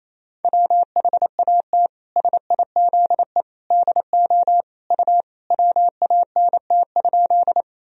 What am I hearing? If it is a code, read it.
Morse code, heard as a keyed tone, about 28 wpm, 700 Hz. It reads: WHAT SIZE DO U WANT?
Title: Sentences from Top 400 Words and Abbreviations 28wpm